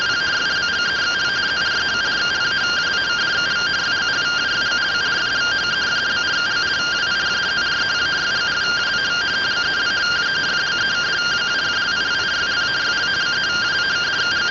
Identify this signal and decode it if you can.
Two-tone teleprinter signal, about 170 Hz shift.